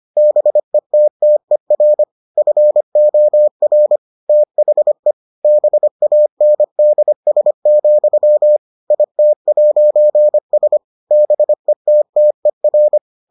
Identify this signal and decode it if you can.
Morse code, heard as a keyed tone, about 25 wpm, 600 Hz. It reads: BETTER FOR THE BANDS, IT'S BETTER